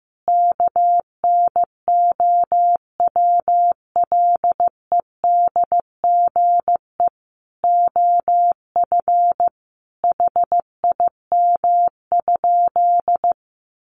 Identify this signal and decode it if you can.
Morse code, heard as a keyed tone, about 15 wpm, 700 Hz. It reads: KNOWLEDGE OF HIM?
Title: Sentences from Top 600 Words 15wpm